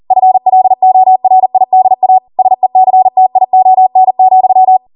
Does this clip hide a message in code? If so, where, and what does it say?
Yes: Morse code from 0.1 to 4.8 s, heard as a keyed tone, about 40 wpm, 750 Hz. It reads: FLORIDA SECTION,